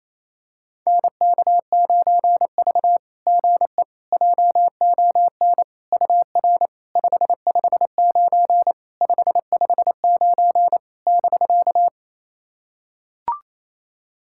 Morse code, heard as a keyed tone, about 27 wpm, 690 Hz. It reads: NK9V GE JON UR 559 559 <BK>
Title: SOTA QSO Practice 28wpm